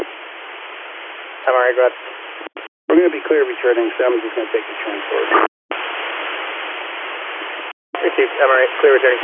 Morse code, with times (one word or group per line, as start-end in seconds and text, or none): none